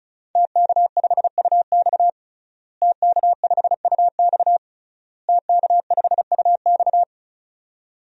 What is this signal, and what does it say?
Morse code, heard as a keyed tone, about 36 wpm, 700 Hz. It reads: TK5UX TK5UX TK5UX
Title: DXCC-100 Call Signs - 5 Characters - Repeated 3x - 35wpm